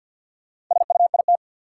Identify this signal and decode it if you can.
Morse code, heard as a keyed tone, about 52 wpm, 700 Hz. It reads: SUIT
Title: Sentences from Top 1600 Words - New Word Drill Alt - 50wpm